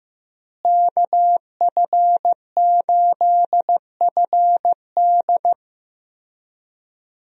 Morse code, heard as a keyed tone, about 15 wpm, 700 Hz. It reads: KF8FD
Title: USA Call Signs - 5 Characters - Repeated 3x - 15wpm